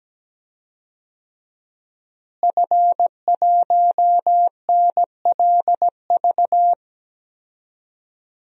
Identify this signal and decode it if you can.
Morse code, heard as a keyed tone, about 17 wpm, 700 Hz. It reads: F1NLV
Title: DXCC-100 Call Signs - 5 Characters - Repeated 3x - 17wpm